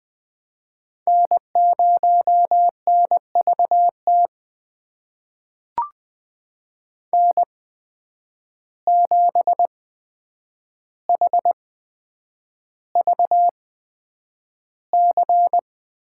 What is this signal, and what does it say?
Morse code, heard as a keyed tone, about 20 wpm, spaced as for 9 wpm, 700 Hz. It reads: N0NVT N7HVC
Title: USA Call Signs - 5 Characters - 8x Character Spacing - 20wpm